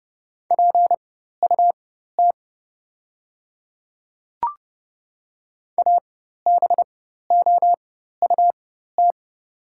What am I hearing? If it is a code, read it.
Morse code, heard as a keyed tone, about 29 wpm, spaced as for 14 wpm, 700 Hz. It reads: PUT ABOUT
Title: 4x Character Spacing - Top 300 Words - 30wpm